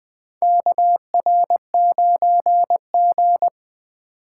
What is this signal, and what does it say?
Morse code, heard as a keyed tone, about 20 wpm, 700 Hz. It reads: KR9G
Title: USA Call Signs - 4 Characters - 3x Character Spacing - 20wpm